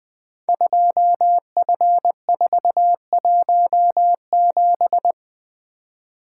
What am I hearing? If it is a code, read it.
Morse code, heard as a keyed tone, about 20 wpm, 700 Hz. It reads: 2F417
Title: Code Groups - 5 Letter:Numbers - 20wpm